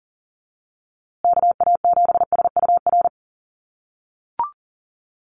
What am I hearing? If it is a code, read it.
Morse code, heard as a keyed tone, about 37 wpm, 700 Hz. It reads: KA7SUR
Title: USA Call Signs Repeated 2x 40wpm